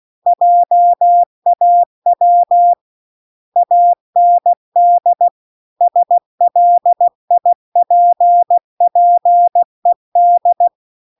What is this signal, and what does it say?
Morse code, heard as a keyed tone, about 16 wpm, 700 Hz. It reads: JAW AND SLIPPED
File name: CW-The-Big-Twoheaded-River-2-16WPM